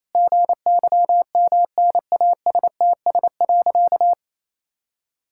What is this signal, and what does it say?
Morse code, heard as a keyed tone, about 28 wpm, 700 Hz. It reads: GYMNASTS.
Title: Pangram Challenge v2 - 28wpm